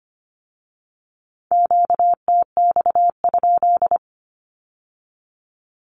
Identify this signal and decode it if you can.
Morse code, heard as a keyed tone, about 25 wpm, 700 Hz. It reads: QTX?
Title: Q-Codes with Spoken Meaning - 25wpm